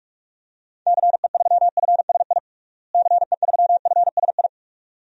Morse code, heard as a keyed tone, about 41 wpm, 700 Hz. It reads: CE3FSI CE3FSI
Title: DXCC-100 Call Signs - 6 Characters - Repeated 2x - 45wpm